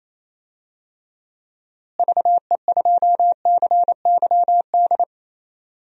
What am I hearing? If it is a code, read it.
Morse code, heard as a keyed tone, about 28 wpm, 700 Hz. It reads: VE2CYD
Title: DXCC-100 Call Signs - 6 Characters - Repeated 2x - 28wpm